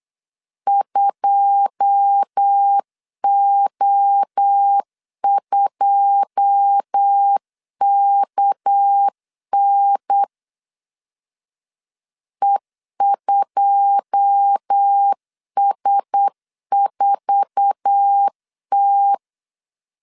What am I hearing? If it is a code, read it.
Morse code, heard as a keyed tone, about 8 wpm, 800 Hz. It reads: 2O2KN E2S4T
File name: LEZIONI / 9 2 8wpm 3437